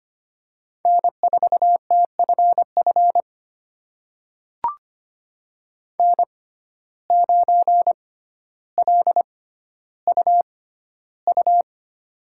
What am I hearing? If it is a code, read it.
Morse code, heard as a keyed tone, about 25 wpm, spaced as for 13 wpm, 700 Hz. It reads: N4TFF N9LUU
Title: USA Call Signs - 5 Characters - 6x Character Spacing - 25wpm